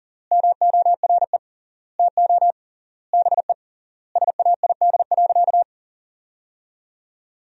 Morse code, heard as a keyed tone, about 40 wpm, 700 Hz. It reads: MORE TO BE SAID.
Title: 3x Word-Spacing - Sentences from Top 100 Words - 40wpm